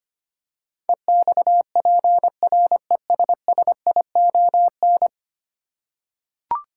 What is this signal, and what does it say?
Morse code, heard as a keyed tone, about 25 wpm, 700 Hz. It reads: EXPRESSION
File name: Ten Letter Words - 25wpm